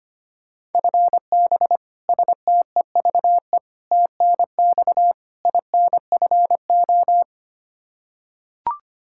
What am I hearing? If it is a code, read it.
Morse code, heard as a keyed tone, about 25 wpm, 700 Hz. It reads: FB STEVE TNX INFO
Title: Practice QSO - Exchange 3 - More Info - 25wpm